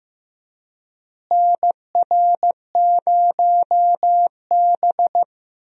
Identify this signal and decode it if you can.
Morse code, heard as a keyed tone, about 16 wpm, 700 Hz. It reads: NR0B
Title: USA Call Signs - 4 Characters - Repeated 3x - 15wpm